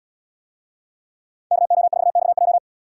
Morse code, heard as a keyed tone, about 77 wpm, 690 Hz. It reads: KO4XJ